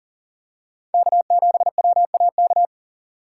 Morse code, heard as a keyed tone, about 40 wpm, 700 Hz. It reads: K7WAK
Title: USA Call Signs - 3x Character Spacing - 40wpm